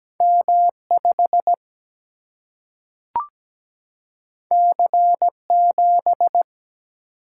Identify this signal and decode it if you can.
Morse code, heard as a keyed tone, about 17 wpm, 700 Hz. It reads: M5 C7
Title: Single-Letter Single-Number 17wpm